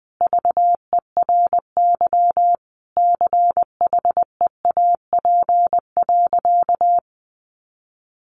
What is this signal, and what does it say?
Morse code, heard as a keyed tone, about 20 wpm, 700 Hz. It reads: VERY CHEAP.
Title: Sentences from Top 1200 Words 20wpm